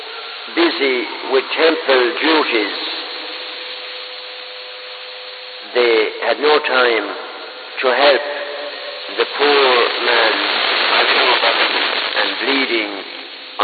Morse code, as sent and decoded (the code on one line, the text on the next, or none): none